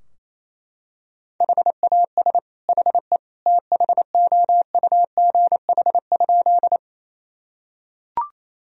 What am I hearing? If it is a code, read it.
Morse code, heard as a keyed tone, about 28 wpm, 700 Hz. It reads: HAS HE THOUGH?
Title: Sentences from Top 1000 Words - Review Part 2 - 28wpm